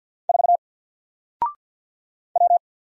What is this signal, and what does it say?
Morse code, heard as a keyed tone, about 48 wpm, 700 Hz. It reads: 4 W
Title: Single Letter-Number Alt 50wpm